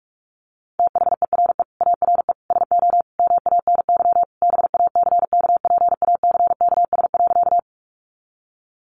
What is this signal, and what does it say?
Morse code, heard as a keyed tone, about 44 wpm, 700 Hz. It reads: THERE ARE SO MANY BACKPACKS.
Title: Additional Easy Phrases 45wpm